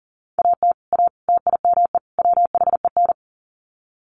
Morse code, heard as a keyed tone, about 40 wpm, 700 Hz. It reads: AT A TIME WHEN
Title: Sets of 4 Words 40wpm